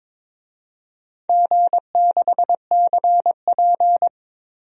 Morse code, heard as a keyed tone, about 22 wpm, 700 Hz. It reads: G6CP
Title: DXCC-100 Call Signs - 4 Characters - 22wpm